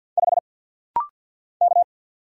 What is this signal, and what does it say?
Morse code, heard as a keyed tone, about 50 wpm, 700 Hz. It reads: 5 K
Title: Single Letter-Number Alt - Rapid-Fire 50wpm